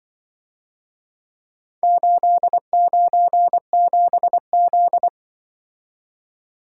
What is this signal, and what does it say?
Morse code, heard as a keyed tone, about 24 wpm, 700 Hz. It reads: 897Z
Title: Speed-Racing - Four Letter-Number Combinations - 20wpm